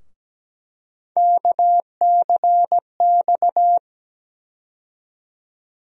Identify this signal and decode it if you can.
Morse code, heard as a keyed tone, about 17 wpm, 700 Hz. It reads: KCX
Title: USA Call Sign Postfixes - 3 Letters 17wpm